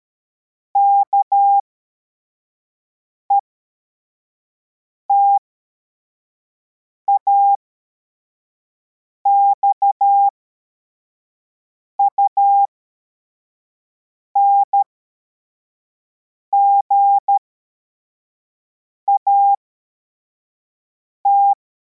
Morse code, heard as a keyed tone, about 13 wpm, spaced as for 4 wpm, 800 Hz. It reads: KETAX UNGAT
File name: formation / cw / Text41